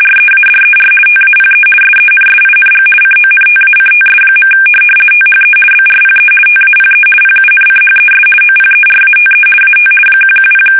RTTY, about 50 baud. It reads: READER TAPE SENSING CONTACTS. THE BRPE TAPE PUNCHES OPERATE AT 110
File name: fsk850-50B / teletype-corp